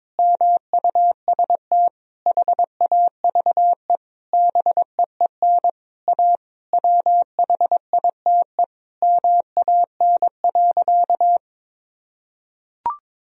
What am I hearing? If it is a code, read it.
Morse code, heard as a keyed tone, about 22 wpm, 700 Hz. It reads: MUST HAVE BEEN A WHITE MAN.
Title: Sentences from Top 1000 Words - Review Part 1 - 22wpm